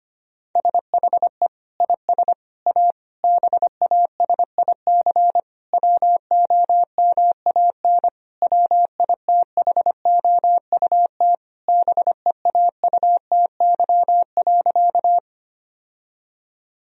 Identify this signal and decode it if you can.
Morse code, heard as a keyed tone, about 25 wpm, 700 Hz. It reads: SHE IS A BASIC WOMAN WITHOUT BEAUTY.
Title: Sentences from Top 1000 Words - Review Part 3 - 25wpm